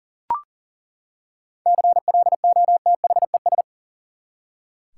Morse code, heard as a keyed tone, about 37 wpm, 700 Hz. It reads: CLOTHES
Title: Seven Letter Words - 40wpm